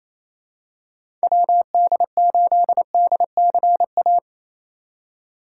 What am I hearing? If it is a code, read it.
Morse code, heard as a keyed tone, about 28 wpm, 700 Hz. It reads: WD8DCA
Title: USA Call Signs - 6 Characters - 4x Character Spacing - 28wpm